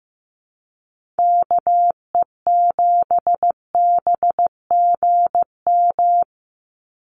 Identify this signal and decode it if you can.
Morse code, heard as a keyed tone, about 15 wpm, 700 Hz. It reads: KE7BGM
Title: NA SOTA Activator Callsigns 15wpm